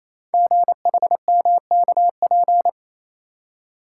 Morse code, heard as a keyed tone, about 27 wpm, 700 Hz. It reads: GHMKP
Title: Code Groups - 5 Letters - 28wpm